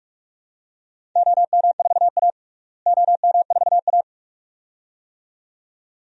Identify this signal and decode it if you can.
Morse code, heard as a keyed tone, about 46 wpm, 700 Hz. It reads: OM4A OM4A
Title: DXCC-100 Call Signs - 4 Characters - Repeated 2x - 45wpm